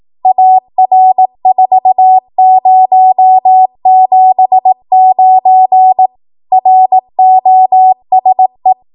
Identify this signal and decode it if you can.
Morse code, heard as a keyed tone, about 18 wpm, 750 Hz. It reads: AR4079 ROSE
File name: Morse / 18WPM